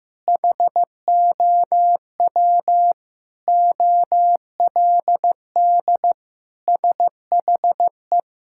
Morse code, heard as a keyed tone, about 16 wpm, 700 Hz. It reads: HOW OLD SHE
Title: Sentences from Top 1600 Words 15wpm